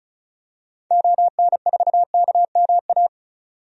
Morse code, heard as a keyed tone, about 36 wpm, 700 Hz. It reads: ON4KMA
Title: DXCC-100 Call Signs - 6 Characters - 35wpm